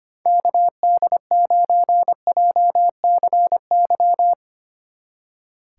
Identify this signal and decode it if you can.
Morse code, heard as a keyed tone, about 31 wpm, 700 Hz. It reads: KD9JCY